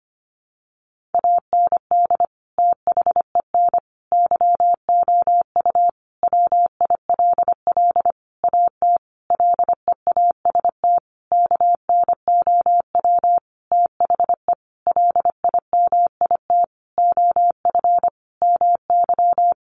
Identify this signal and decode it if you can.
Morse code, heard as a keyed tone, about 25 wpm, 700 Hz. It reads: AND THEN YOU WILL AT LEAST KNOW THE LIMIT OF MY
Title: Sentences from Top 1500 Words 25wpm